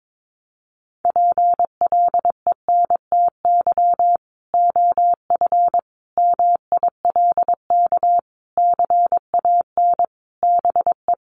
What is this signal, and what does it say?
Morse code, heard as a keyed tone, about 22 wpm, 700 Hz. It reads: PLENTY OF MILK CAN BE
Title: Sentences from Top 1500 Words 22wpm